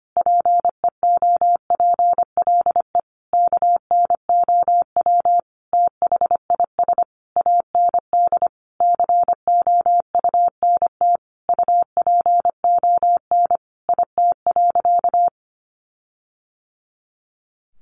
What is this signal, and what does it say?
Morse code, heard as a keyed tone, about 25 wpm, 700 Hz. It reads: PEOPLE KNOW THIS AND COUNT UPON IT.
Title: Sentences from Top 1500 Words 25wpm